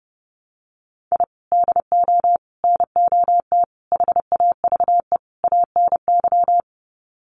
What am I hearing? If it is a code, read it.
Morse code, heard as a keyed tone, about 30 wpm, 700 Hz. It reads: I DO NOT HAVE ANY
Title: Sets of 5 Words 30wpm